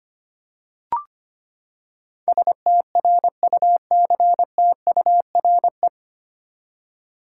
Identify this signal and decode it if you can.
Morse code, heard as a keyed tone, about 25 wpm, 700 Hz. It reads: STRUCTURE
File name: Nine Letter Words - 25wpm